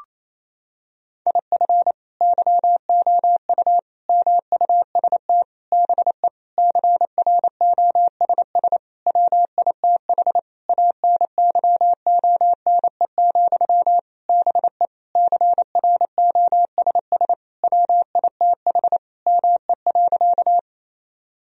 Morse code, heard as a keyed tone, about 28 wpm, 700 Hz. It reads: IF YOU MUST BE CROSS WITH ANYONE, BE CROSS WITH ME.
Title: Sentences from Top 1000 Words - Review Part 1 - 28wpm